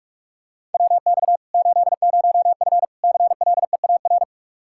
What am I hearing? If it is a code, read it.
Morse code, heard as a keyed tone, about 46 wpm, 700 Hz. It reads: WX 80F CLEAR
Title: Practice QSO - Exchange 2 - Info - 45wpm